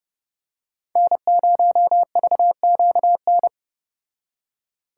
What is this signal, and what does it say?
Morse code, heard as a keyed tone, about 30 wpm, 700 Hz. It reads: N0VQN